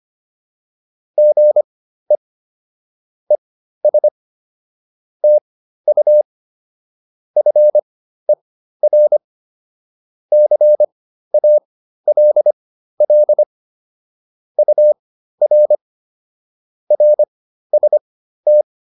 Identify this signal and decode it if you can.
Morse code, heard as a keyed tone, about 25 wpm, spaced as for 13 wpm, 600 Hz. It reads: GE ES TU FER CALL UR RST